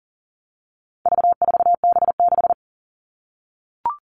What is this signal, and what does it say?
Morse code, heard as a keyed tone, about 40 wpm, 700 Hz. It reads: V4B6
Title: All Cut Numbers - 4-Digit 40wpm